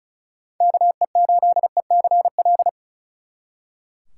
Morse code, heard as a keyed tone, about 46 wpm, 700 Hz. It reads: KE8ECL